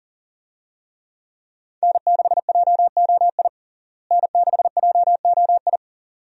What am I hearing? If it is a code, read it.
Morse code, heard as a keyed tone, about 40 wpm, 700 Hz. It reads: N6JOI N6JOI
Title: USA Call Signs - 5 Characters - Repeated 2x - 40wpm